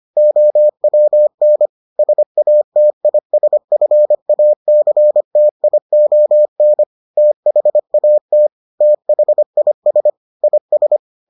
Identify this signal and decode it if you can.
Morse code, heard as a keyed tone, about 25 wpm, 600 Hz. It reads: OWN SATISFACTION THAT THIS IS